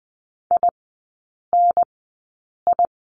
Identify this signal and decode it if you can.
Morse code, heard as a keyed tone, about 20 wpm, 700 Hz. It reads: I N I
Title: 029 Lesson 6e - Letter i - Three Characters 20wpm